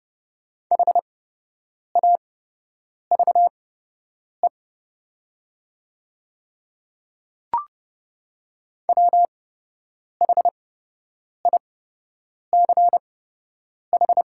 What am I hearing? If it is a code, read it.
Morse code, heard as a keyed tone, about 30 wpm, spaced as for 8 wpm, 700 Hz. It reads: HAVE WHICH
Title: 8x Character Spacing - Top 100 Words - 30wpm